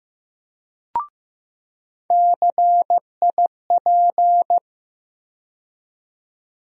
Morse code, heard as a keyed tone, about 15 wpm, 700 Hz. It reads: CIP
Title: USA Call Sign Postfixes - 3 Letters 15wpm